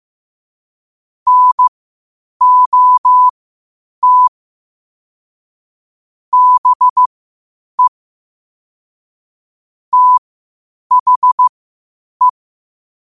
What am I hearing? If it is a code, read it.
Morse code, heard as a keyed tone, about 15 wpm, spaced as for 8 wpm, 1000 Hz. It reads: NOT BE THE
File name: qotdfiles / 8wpm